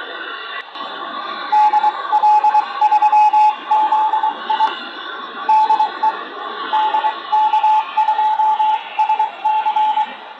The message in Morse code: DL3ZI DE DK2SC